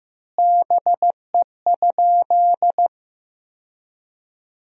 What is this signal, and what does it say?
Morse code, heard as a keyed tone, about 15 wpm, 700 Hz. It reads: BE?
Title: 4x Word-Spacing - Sentences from Top 100 Words - 15wpm